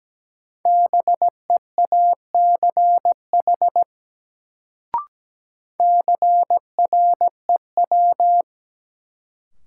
Morse code, heard as a keyed tone, about 17 wpm, 700 Hz. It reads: BEACH CREW
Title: Sentences from Top 1500 Words - New Word Drill Alt - 17wpm